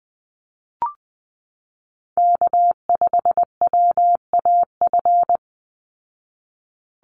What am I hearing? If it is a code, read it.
Morse code, heard as a keyed tone, about 20 wpm, 700 Hz. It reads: K5WAF
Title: Call Signs Repeated 2x - 20wpm